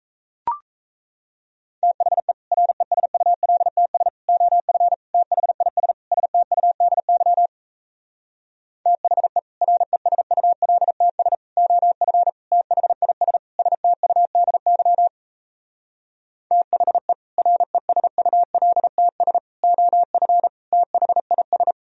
Morse code, heard as a keyed tone, about 37 wpm, 700 Hz. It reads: THE RESULTS OF THIS STUDY THE RESULTS OF THIS STUDY THE RESULTS OF THIS